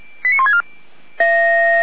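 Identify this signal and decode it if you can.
Five-tone selective call (ZVEI), heard as quick runs of five tones, tones of about 70 ms. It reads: 89164
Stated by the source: Five-tone ZVEI selective call received on an SDR.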